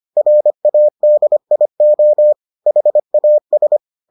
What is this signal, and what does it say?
Morse code, heard as a keyed tone, about 25 wpm, 600 Hz. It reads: RADIO HAS